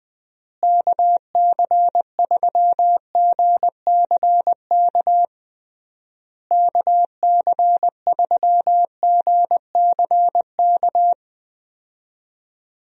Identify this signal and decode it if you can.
Morse code, heard as a keyed tone, about 21 wpm, 700 Hz. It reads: KC3GCK KC3GCK
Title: USA Call Signs - 6 Characters - Repeated 3x - 20wpm